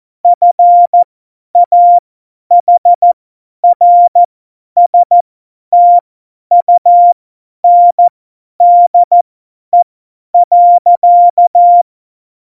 Morse code, heard as a keyed tone, about 14 wpm, spaced as for 10 wpm, 700 Hz. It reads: FAHRSTUNDE.
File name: Morsekurs / Witze / Witz 08. Januar 2019, 50 F70